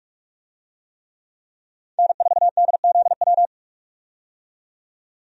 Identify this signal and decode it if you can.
Morse code, heard as a keyed tone, about 43 wpm, 700 Hz. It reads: N4DZW
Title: USA Call Signs - 5 Characters - 45wpm